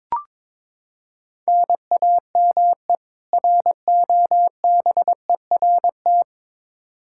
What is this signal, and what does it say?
Morse code, heard as a keyed tone, about 22 wpm, 700 Hz. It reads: NAME ROBERT
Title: Practice QSO - Exchange 1 - Beginning - 22wpm